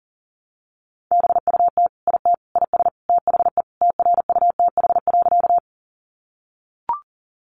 Morse code, heard as a keyed tone, about 40 wpm, 690 Hz. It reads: BUT IT IS THE TRUTH.
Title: Sentences from Top 800 Words 40wpm